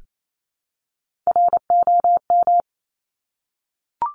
Morse code, heard as a keyed tone, about 28 wpm, 700 Hz. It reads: ROM
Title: Top Three Letter Combinations in Words 28wpm